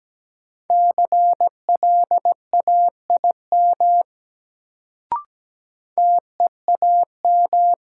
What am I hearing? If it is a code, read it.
Morse code, heard as a keyed tone, about 17 wpm, 700 Hz. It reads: CLAIM TEAM
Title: Sentences from Top 1500 Words - New Word Drill - 17wpm